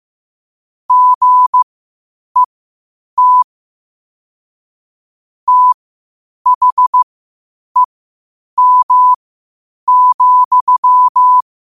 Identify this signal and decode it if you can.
Morse code, heard as a keyed tone, about 15 wpm, spaced as for 8 wpm, 1000 Hz. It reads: GET THEM,
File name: qotdfiles / 8wpm